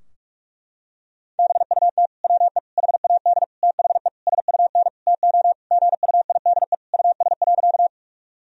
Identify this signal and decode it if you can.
Morse code, heard as a keyed tone, about 47 wpm, 700 Hz. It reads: BUT WE HAD THE SUN TO GUIDE US.
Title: Sentences from Top 1000 Words - Review Part 3 - 45wpm